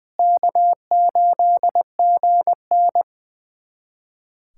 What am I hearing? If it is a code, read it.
Morse code, heard as a keyed tone, about 20 wpm, 700 Hz. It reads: K8GN
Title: USA Callsign DE Callsign 20wpm